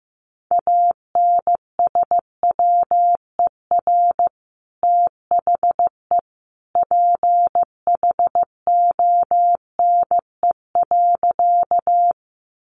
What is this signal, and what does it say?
Morse code, heard as a keyed tone, about 15 wpm, 700 Hz. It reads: ANSWER THE PHONE.
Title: Sentences from Top 1100 Words 15wpm